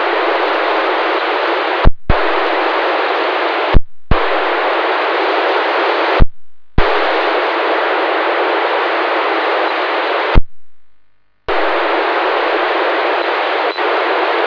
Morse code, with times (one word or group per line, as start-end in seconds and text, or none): none